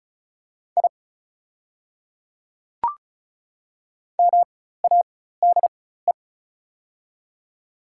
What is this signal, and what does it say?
Morse code, heard as a keyed tone, about 36 wpm, spaced as for 16 wpm, 700 Hz. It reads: I MADE
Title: 4x Character Spacing - Top 200 Words - 35wpm